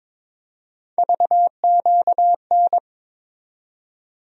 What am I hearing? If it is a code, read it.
Morse code, heard as a keyed tone, about 21 wpm, 700 Hz. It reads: VQN
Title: USA Call Sign Postfixes - 3 Letters 22wpm